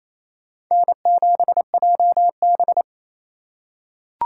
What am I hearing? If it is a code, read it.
Morse code, heard as a keyed tone, about 28 wpm, 700 Hz. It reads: N7JB
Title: USA Call Signs - 4 Characters - 4x Character Spacing - 28wpm